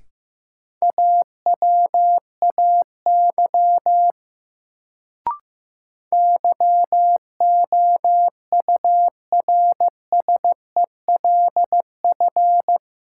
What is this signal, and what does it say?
Morse code, heard as a keyed tone, about 15 wpm, 700 Hz. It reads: AWAY YOURSELF
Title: Sentences from Top 600 Words - New Word Drill - 15wpm